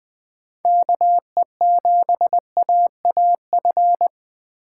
Morse code, heard as a keyed tone, about 20 wpm, 700 Hz. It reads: KE7AAF